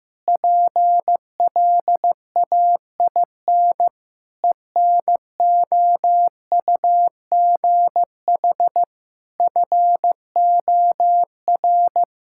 Morse code, heard as a keyed tone, about 15 wpm, 700 Hz. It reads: PLAIN ENOUGH FOR
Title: Sentences from Top 1000 Words - Review Part 5 - 15wpm